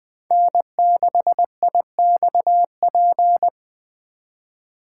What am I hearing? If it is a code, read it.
Morse code, heard as a keyed tone, about 24 wpm, 700 Hz. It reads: N6IXP
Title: Speed Racing - USA 1x3 Call Signs 20wpm